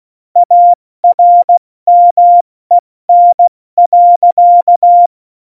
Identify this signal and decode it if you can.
Morse code, heard as a keyed tone, about 16 wpm, spaced as for 14 wpm, 700 Hz. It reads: ARMEN.
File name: Morsekurs / Witze / Witz 06. Februar 2018, 70 F80